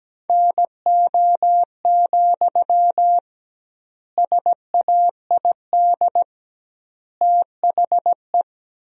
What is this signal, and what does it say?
Morse code, heard as a keyed tone, about 17 wpm, 700 Hz. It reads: NO, SAID THE
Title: 2x Word-Spacing - Sentences from Top 300 Words - 17wpm